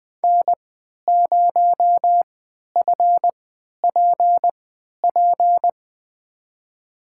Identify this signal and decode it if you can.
Morse code, heard as a keyed tone, about 20 wpm, spaced as for 11 wpm, 700 Hz. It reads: N0FPP
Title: USA Call Signs - 3x Character Spacing - 20wpm